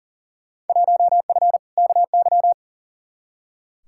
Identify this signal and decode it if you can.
Morse code, heard as a keyed tone, about 38 wpm, 700 Hz. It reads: 1F KY
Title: Field Day Exchanges Encore - 40wpm